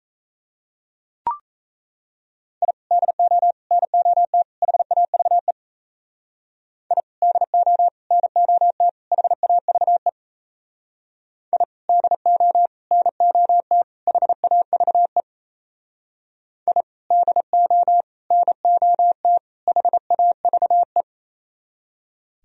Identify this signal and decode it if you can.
Morse code, heard as a keyed tone, about 34 wpm, 700 Hz. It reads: I DO NOT HAVE I DO NOT HAVE I DO NOT HAVE I DO NOT HAVE
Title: Speed-Racing - Sets of 4 Words 28wpm